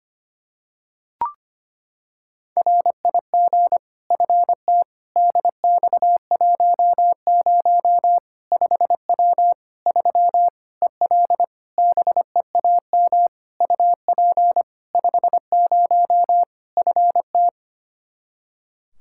Morse code, heard as a keyed tone, about 25 wpm, 700 Hz. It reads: RIG FT DX10 5W 3 EL BEAM UP 50 FT
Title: Practice QSO - Exchange 2 - Info - 25wpm